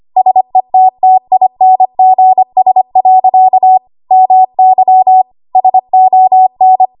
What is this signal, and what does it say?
Morse code, heard as a keyed tone, about 25 wpm, 750 Hz. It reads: SETTINGS. MY SON
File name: Archive / 25 WPM / 25WPM